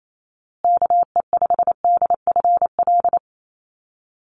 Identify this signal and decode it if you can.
Morse code, heard as a keyed tone, about 28 wpm, 700 Hz. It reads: KE5DFL
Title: USA Call Signs Repeated 3x 28wpm